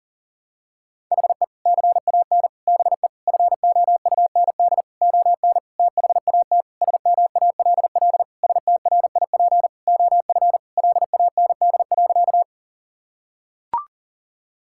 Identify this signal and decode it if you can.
Morse code, heard as a keyed tone, about 40 wpm, 700 Hz. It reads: HE CAN BE FOUND ON THAT SMALL STRIP OF LAND.
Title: Sentences from Top 1600 Words 40wpm